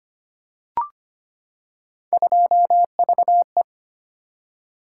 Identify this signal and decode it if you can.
Morse code, heard as a keyed tone, about 25 wpm, 700 Hz. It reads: 2VE
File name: Three Letter-Number Combinations - 25wpm